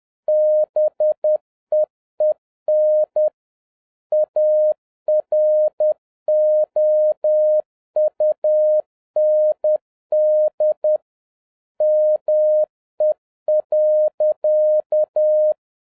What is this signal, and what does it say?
Morse code, heard as a keyed tone, about 10 wpm, 610 Hz. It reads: BEEN AROUND ME.
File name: 2011-07-mp3 / n56 BACK FROM A BIG ACCIDENT 10wpm